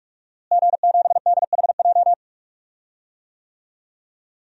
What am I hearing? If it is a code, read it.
Morse code, heard as a keyed tone, about 41 wpm, 700 Hz. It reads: G7DHJ
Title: DXCC-100 Call Signs - 5 Characters - Repeated 3x - 45wpm